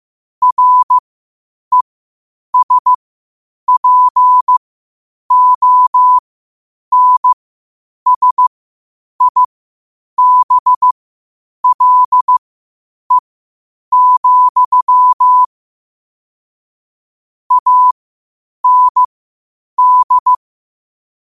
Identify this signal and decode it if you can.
Morse code, heard as a keyed tone, about 15 wpm, spaced as for 8 wpm, 1000 Hz. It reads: RESPONSIBLE, AND
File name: qotdfiles / 8wpm